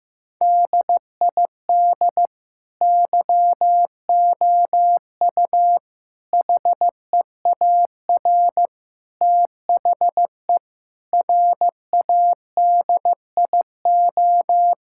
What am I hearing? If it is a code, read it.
Morse code, heard as a keyed tone, about 15 wpm, 700 Hz. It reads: DID YOU HEAR THE RADIO
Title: Sentences from Top 1000 Words - Review Part 5 - 15wpm